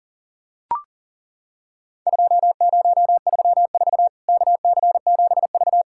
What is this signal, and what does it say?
Morse code, heard as a keyed tone, about 39 wpm, 700 Hz. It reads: 2034 KC7V
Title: Worked All Europe DX Contest - QTC Exchange - 40wpm